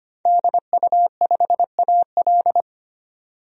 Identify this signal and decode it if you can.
Morse code, heard as a keyed tone, about 25 wpm, 700 Hz. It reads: DU5AL
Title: DXCC-100 Call Signs - 5 Characters - 25wpm